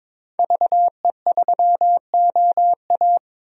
Morse code, heard as a keyed tone, about 22 wpm, 700 Hz. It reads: VE3OA
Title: DXCC-100 Call Signs - 5 Characters - 22wpm